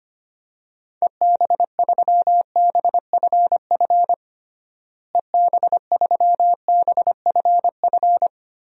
Morse code, heard as a keyed tone, about 25 wpm, 700 Hz. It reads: EB3BFF EB3BFF
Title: DXCC-100 Call Signs - 6 Characters - Repeated 3x - 25wpm